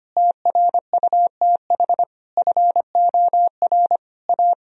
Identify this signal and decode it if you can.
Morse code, heard as a keyed tone, about 25 wpm, 700 Hz. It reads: TRUTH FOR A
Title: Sentences from Top 1000 Words - Review Part 2 - 25wpm